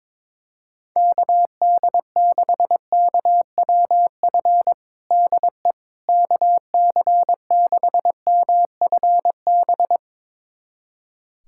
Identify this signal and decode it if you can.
Morse code, heard as a keyed tone, about 22 wpm, 700 Hz. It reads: KD6KWF DE KC6MFB
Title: USA Callsign DE Callsign - 6 Characters - 22wpm